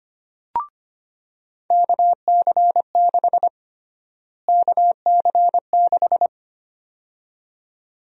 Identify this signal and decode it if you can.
Morse code, heard as a keyed tone, about 25 wpm, 700 Hz. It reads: KC6 KC6
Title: USA Call Sign Prefixes - Repeated 2x - 25wpm